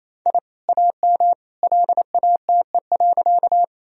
Morse code, heard as a keyed tone, about 28 wpm, 700 Hz. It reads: I AM LATE.
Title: Sentences from Top 1000 Words - Review Part 2 - 28wpm